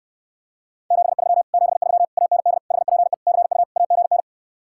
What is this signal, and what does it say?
Morse code, heard as a keyed tone, about 69 wpm, 700 Hz. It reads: 73 73 AND HPE CU AGN